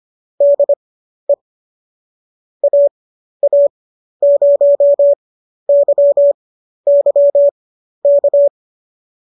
Morse code, heard as a keyed tone, about 25 wpm, spaced as for 12 wpm, 570 Hz. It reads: DE AA0YY K
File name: qso008 12